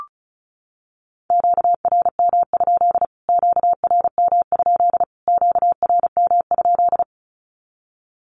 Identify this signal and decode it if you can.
Morse code, heard as a keyed tone, about 34 wpm, 700 Hz. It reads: QRM? QRM? QRM?
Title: Q-Codes Repeated 3x 35wpm